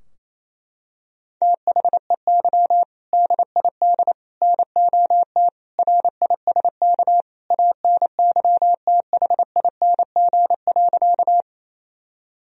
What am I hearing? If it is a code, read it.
Morse code, heard as a keyed tone, about 28 wpm, 700 Hz. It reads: THEY DID NOT RISK ANYTHING.
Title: Sentences from Top 1000 Words - Review Part 4 - 28wpm